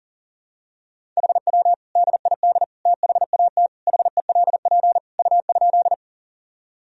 Morse code, heard as a keyed tone, about 37 wpm, 700 Hz. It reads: HW DID THAT HELP U?
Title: Sentences from Top 200 Words and Abbreviations 40wpm